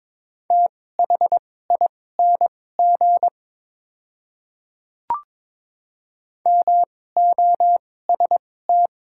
Morse code, heard as a keyed tone, about 22 wpm, spaced as for 16 wpm, 700 Hz. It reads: THING MOST
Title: 2x Character Spacing - Top 100 Words - 22wpm